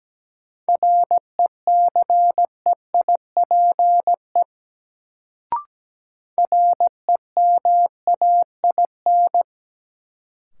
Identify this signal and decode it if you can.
Morse code, heard as a keyed tone, about 17 wpm, 700 Hz. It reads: RECEIPE REMAIN
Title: Sentences from Top 1200 Words - New Word Drill Alt - 17wpm